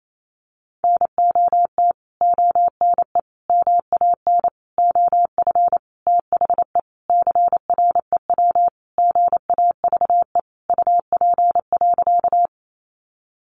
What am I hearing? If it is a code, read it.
Morse code, heard as a keyed tone, about 28 wpm, 700 Hz. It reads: NOT ONE MAN OF THE CREW GAVE UP.
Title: Sentences from Top 1500 Words 28wpm